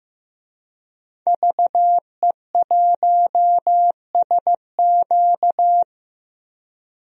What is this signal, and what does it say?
Morse code, heard as a keyed tone, about 15 wpm, 700 Hz. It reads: VE1SQ
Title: DXCC-100 Call Signs - 5 Characters - Repeated 2x - 15wpm